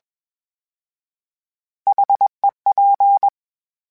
Morse code, heard as a keyed tone, about 21 wpm, 800 Hz. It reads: HEP